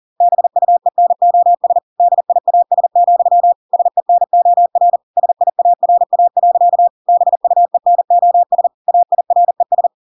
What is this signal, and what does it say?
Morse code, heard as a keyed tone, about 40 wpm, 700 Hz. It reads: BUENOS DIAS, SENOR SIARA. BUENOS AIRES